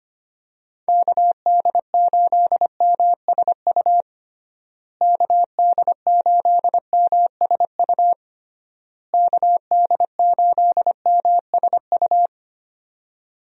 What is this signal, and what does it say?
Morse code, heard as a keyed tone, about 25 wpm, 700 Hz. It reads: KD8MSU KD8MSU KD8MSU
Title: USA Call Signs - 6 Characters - Repeated 3x - 25wpm